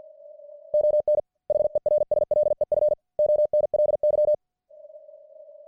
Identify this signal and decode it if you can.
Morse code, heard as a keyed tone, about 50 wpm, 600 Hz. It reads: ON HERSLEF ONLY